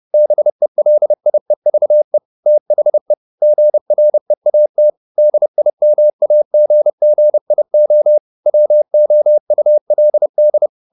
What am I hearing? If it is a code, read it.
Morse code, heard as a keyed tone, about 30 wpm, 600 Hz. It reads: BELIEVE THE GREAT DIMAGGIO WOULD
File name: the old man and the sea / the old man and the sea 1 30wpm